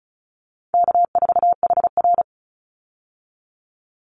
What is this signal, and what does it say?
Morse code, heard as a keyed tone, about 34 wpm, 700 Hz. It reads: K4HR
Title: Easy Call Signs Repeated 3x - 35wpm